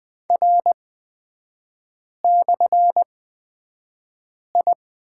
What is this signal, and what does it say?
Morse code, heard as a keyed tone, about 20 wpm, spaced as for 5 wpm, 690 Hz. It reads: R/I
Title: 229 Lesson 37e - Letter x - Three Characters 20wpm